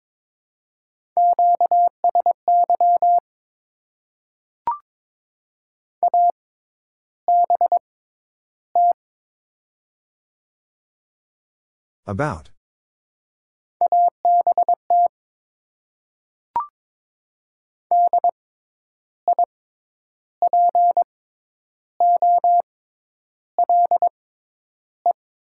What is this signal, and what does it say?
Morse code, heard as a keyed tone, about 22 wpm, spaced as for 10 wpm, 700 Hz. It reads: QSY ABT ABT DIPOLE
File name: Top 90 QSO Elements - 6x Character Spacing - 22wpm